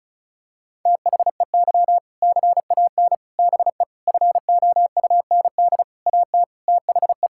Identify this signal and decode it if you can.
Morse code, heard as a keyed tone, about 34 wpm, 700 Hz. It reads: THEY CAN BE FOUND AT THE
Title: Sentences from Top 1600 Words 35wpm